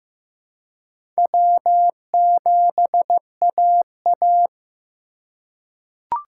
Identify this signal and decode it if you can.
Morse code, heard as a keyed tone, about 15 wpm, 700 Hz. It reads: W7AA
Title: USA Call Signs - 4 Characters - 3x Character Spacing - 15wpm